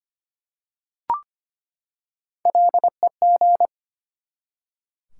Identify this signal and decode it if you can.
Morse code, heard as a keyed tone, about 25 wpm, 700 Hz. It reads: LEG
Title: Three Letter Words - 25wpm